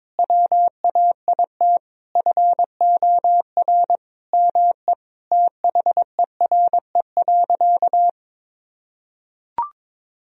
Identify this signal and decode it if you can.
Morse code, heard as a keyed tone, about 22 wpm, 700 Hz. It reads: WAIT FOR ME THERE.
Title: Sentences from Top 1000 Words - Review Part 4 - 22wpm